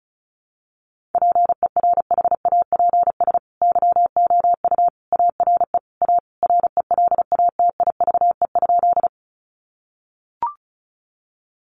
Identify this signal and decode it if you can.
Morse code, heard as a keyed tone, about 34 wpm, 700 Hz. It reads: PERHAPS YOU ARE A RELATIVE?
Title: Sentences from Top 1400 Words 35wpm